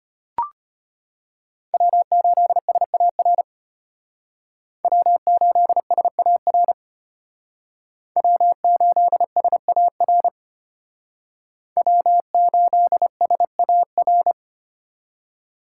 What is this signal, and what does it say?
Morse code, heard as a keyed tone, about 32 wpm, 700 Hz. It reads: W8SAR W8SAR W8SAR W8SAR